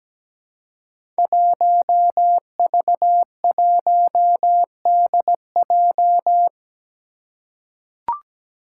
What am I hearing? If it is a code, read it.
Morse code, heard as a keyed tone, about 17 wpm, 700 Hz. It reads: 1V1DJ
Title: Five Challenging Letter-Number Combinations 17wpm